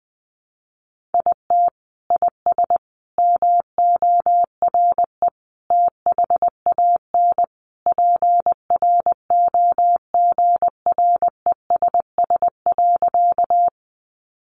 Morse code, heard as a keyed tone, about 20 wpm, 700 Hz. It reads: IT IS MORE THAN PROGRESS.
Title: Sentences from Top 1400 Words 20wpm